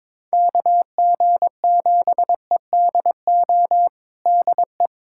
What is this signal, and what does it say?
Morse code, heard as a keyed tone, about 21 wpm, 700 Hz. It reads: KG7EDO DE
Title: USA Callsign DE Callsign 22wpm